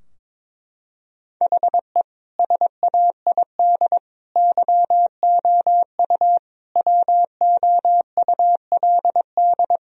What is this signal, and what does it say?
Morse code, heard as a keyed tone, about 22 wpm, 700 Hz. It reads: HE SAID YOU WOULD
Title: Sentences from Top 1000 Words - Review Part 2 - 22wpm